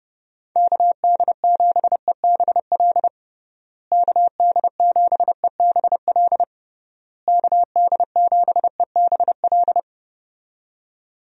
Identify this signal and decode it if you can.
Morse code, heard as a keyed tone, about 29 wpm, 700 Hz. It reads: KD7EBL KD7EBL KD7EBL
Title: USA Call Signs - 6 Characters - Repeated 3x - 30wpm